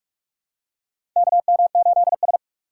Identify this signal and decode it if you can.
Morse code, heard as a keyed tone, about 46 wpm, 700 Hz. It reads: KM8S
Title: USA Call Signs - 2x Character Spacing - 45wpm